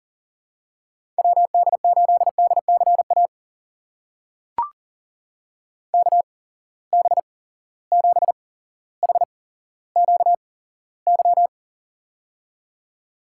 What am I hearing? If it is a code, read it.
Morse code, heard as a keyed tone, about 40 wpm, spaced as for 17 wpm, 700 Hz. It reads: WD8DCA KB7HQY
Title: USA Call Signs - 6 Characters - 8x Character Spacing - 40wpm